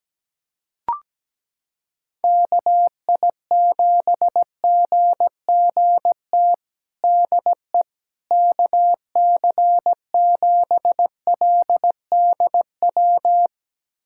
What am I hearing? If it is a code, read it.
Morse code, heard as a keyed tone, about 17 wpm, 700 Hz. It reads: KI7GGT DE KC7LDW
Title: USA Callsign DE Callsign - 6 Characters - 17wpm